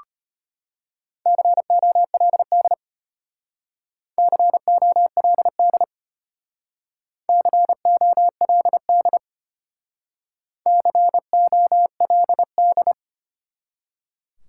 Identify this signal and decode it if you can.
Morse code, heard as a keyed tone, about 34 wpm, 700 Hz. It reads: COLD COLD COLD COLD